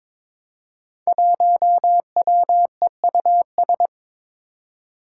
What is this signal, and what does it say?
Morse code, heard as a keyed tone, about 22 wpm, 700 Hz. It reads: 1WEUS
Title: Five Challenging Letter-Number Combinations 22wpm